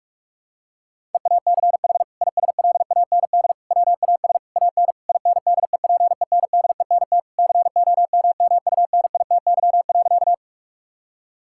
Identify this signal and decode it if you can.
Morse code, heard as a keyed tone, about 41 wpm, 700 Hz. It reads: EACH ISLAND WAS AN INDEPENDENT COMMUNITY.
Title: Sentences from Top 1000 Words - Review Part 6 - 45wpm